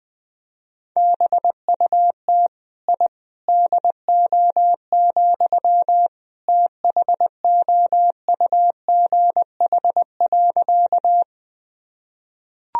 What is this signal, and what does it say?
Morse code, heard as a keyed tone, about 21 wpm, 700 Hz. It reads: BUT I DO, THOUGH.
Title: Sentences from Top 1000 Words - Review Part 2 - 20wpm